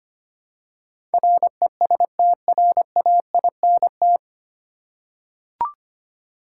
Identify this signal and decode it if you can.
Morse code, heard as a keyed tone, about 25 wpm, 700 Hz. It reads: RESTRAINT
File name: Nine Letter Words - 25wpm